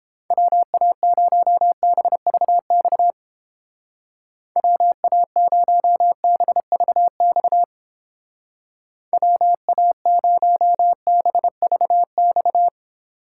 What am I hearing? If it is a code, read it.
Morse code, heard as a keyed tone, about 29 wpm, 700 Hz. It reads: WA0BVX WA0BVX WA0BVX